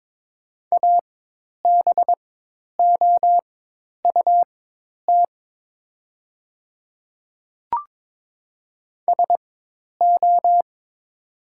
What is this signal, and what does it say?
Morse code, heard as a keyed tone, about 22 wpm, spaced as for 10 wpm, 690 Hz. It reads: ABOUT SO
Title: 4x Character Spacing - Top 100 Words - 22wpm